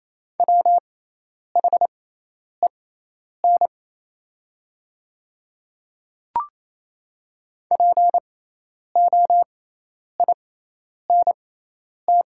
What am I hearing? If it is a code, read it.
Morse code, heard as a keyed tone, about 28 wpm, spaced as for 10 wpm, 700 Hz. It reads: WHEN POINT
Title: 6x Character Spacing - Top 300 Words - 28wpm